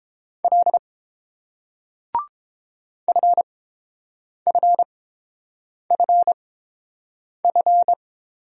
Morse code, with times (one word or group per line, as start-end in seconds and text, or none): 0.4-0.8 L
3.1-7.9 FFFF